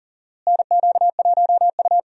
Morse code, heard as a keyed tone, about 38 wpm, 700 Hz. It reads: NQ1U
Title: USA Call Signs - 4 Characters - Repeated 2x - 40wpm